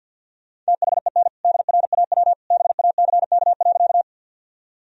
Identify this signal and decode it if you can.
Morse code, heard as a keyed tone, about 49 wpm, 700 Hz. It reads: THEN DRAW BACK.
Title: Sentences from Top 1000 Words - Review Part 1 - 50wpm